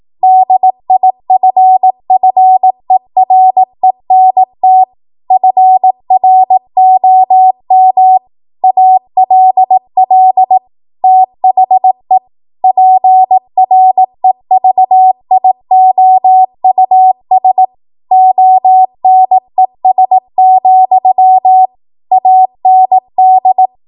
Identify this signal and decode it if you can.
Morse code, heard as a keyed tone, about 18 wpm, 750 Hz. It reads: DIFFERENT FROM ALL THE PREVIOUS ONES, AND
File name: Morse / 18WPM